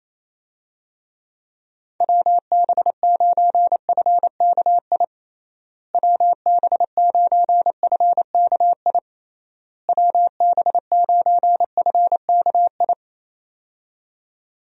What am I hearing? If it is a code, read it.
Morse code, heard as a keyed tone, about 27 wpm, 700 Hz. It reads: WB9FKI WB9FKI WB9FKI
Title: USA Call Signs - 6 Characters - Repeated 3x - 28wpm